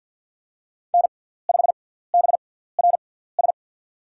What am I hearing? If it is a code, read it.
Morse code, heard as a keyed tone, about 53 wpm, spaced as for 17 wpm, 700 Hz. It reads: N5BRS
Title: USA Call Signs - 6x Character Spacing - 50wpm